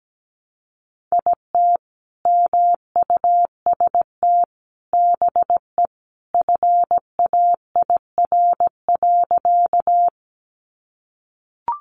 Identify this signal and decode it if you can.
Morse code, heard as a keyed tone, about 17 wpm, 700 Hz. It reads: IT MUST BE FAIR.
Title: Sentences from Top 1300 Words 17wpm